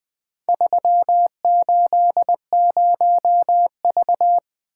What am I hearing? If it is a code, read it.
Morse code, heard as a keyed tone, about 23 wpm, 700 Hz. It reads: 380V
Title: Speed-Racing - Four Letter-Number Combinations - 20wpm